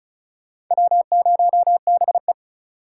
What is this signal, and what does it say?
Morse code, heard as a keyed tone, about 36 wpm, 700 Hz. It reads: W0BE
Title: USA Call Signs - 4 Characters - 4x Character Spacing - 35wpm